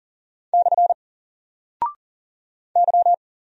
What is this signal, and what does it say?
Morse code, heard as a keyed tone, about 40 wpm, 700 Hz. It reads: / Y
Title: Single Character (Letter:Number:Punctuation:Prosign) 40wpm